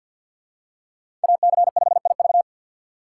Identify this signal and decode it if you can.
Morse code, heard as a keyed tone, about 47 wpm, 700 Hz. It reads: AC5IV
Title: USA Call Signs - 5 Characters - 6x Character Spacing - 50wpm